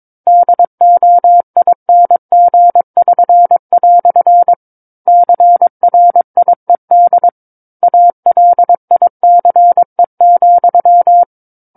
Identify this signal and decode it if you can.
Morse code, heard as a keyed tone, about 22 wpm, 700 Hz. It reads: DOING<SN>" CRIED ALICE,
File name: 06-Chapter-06 22 WPM